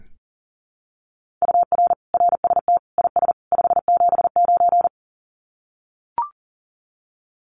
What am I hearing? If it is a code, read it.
Morse code, heard as a keyed tone, about 40 wpm, 700 Hz. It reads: UR RST IS 579
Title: Common QSO Phrases 40wpm